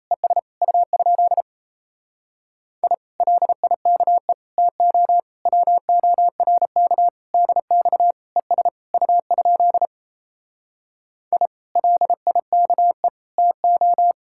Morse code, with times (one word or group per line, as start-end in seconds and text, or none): 0.1-0.4 ES
0.6-1.4 U?
2.8-2.9 I
3.2-4.3 LIKE
4.6-5.2 TO
5.5-7.1 WORK
7.3-8.1 DX
8.4-8.7 ES
8.9-9.9 U?
11.3-11.5 I
11.8-13.1 LIKE
13.4-14.1 TO